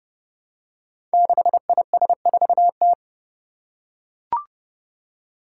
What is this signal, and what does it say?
Morse code, heard as a keyed tone, about 30 wpm, 700 Hz. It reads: 6IS4T
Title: Five Challenging Letter-Number Combinations 30wpm